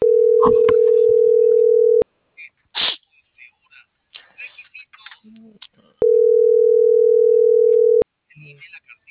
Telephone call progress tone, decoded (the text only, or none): ringback tone